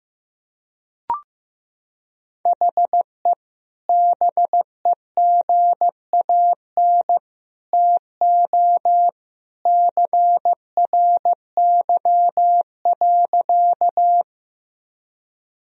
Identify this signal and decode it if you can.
Morse code, heard as a keyed tone, about 15 wpm, 700 Hz. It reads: HE BEGAN TO CRY.
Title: Sentences from Top 1000 Words - Review Part 2 - 15wpm